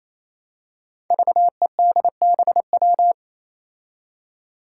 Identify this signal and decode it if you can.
Morse code, heard as a keyed tone, about 27 wpm, 700 Hz. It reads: VEDBW
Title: Five Challenging Letter-Number Combinations 28wpm